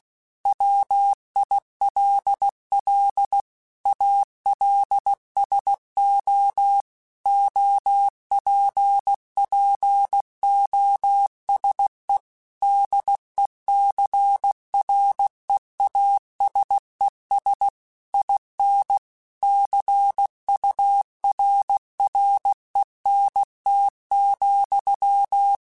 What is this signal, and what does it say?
Morse code, heard as a keyed tone, about 16 wpm, 800 Hz. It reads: WILL ALSO OPPOSE DECREASES IN CURRENT,
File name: Technical Subjects / Technical Subjects-mp3s / XL @15